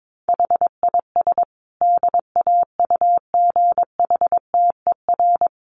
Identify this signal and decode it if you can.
Morse code, heard as a keyed tone, about 22 wpm, 700 Hz. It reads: HIS DAUGHTER
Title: Sentences from Top 1500 Words 22wpm